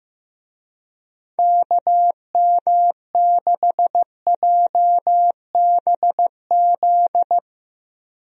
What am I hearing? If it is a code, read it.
Morse code, heard as a keyed tone, about 15 wpm, 700 Hz. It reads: KM6JBZ
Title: USA Call Signs - 6 Characters - Repeated 2x - 15wpm